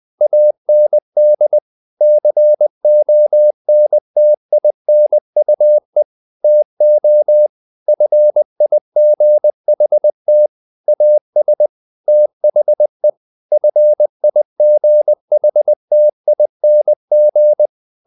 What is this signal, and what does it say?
Morse code, heard as a keyed tone, about 20 wpm, 600 Hz. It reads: AND CONTINUE TO FIGHT AS THE FIGHTING
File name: the old man and the sea / the old man and the sea part 1 20wpm